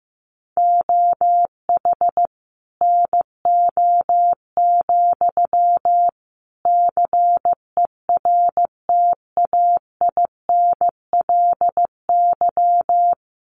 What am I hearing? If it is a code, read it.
Morse code, heard as a keyed tone, about 15 wpm, 700 Hz. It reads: OH NO, CERTAINLY
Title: Sentences from Top 900 Words 15wpm